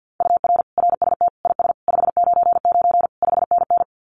Morse code, heard as a keyed tone, about 50 wpm, 700 Hz. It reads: UR RST IS 599 5NN
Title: Common QSO Phrases Encore 50wpm